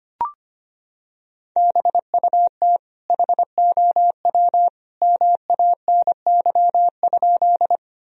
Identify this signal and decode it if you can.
Morse code, heard as a keyed tone, about 25 wpm, 700 Hz. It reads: BUT HOW MANY?
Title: Sentences from Top 1000 Words - Review Part 1 - 25wpm